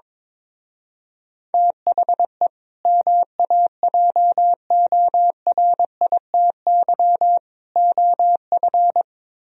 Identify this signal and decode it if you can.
Morse code, heard as a keyed tone, about 22 wpm, 700 Hz. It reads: THE MAJORITY OF
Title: Speed-Racing - Sets of 3 Words 22wpm